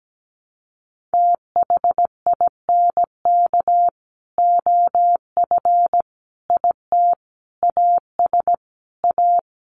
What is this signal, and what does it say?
Morse code, heard as a keyed tone, about 17 wpm, 700 Hz. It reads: THINK OF IT AS A
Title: Sets of 5 Words Encore2 17wpm